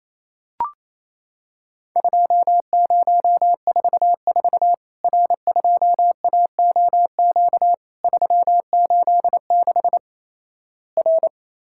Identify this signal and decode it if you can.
Morse code, heard as a keyed tone, about 28 wpm, 700 Hz. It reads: 2044 R2AOQ 386 R
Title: Worked All Europe DX Contest - QTC Exchange - 28wpm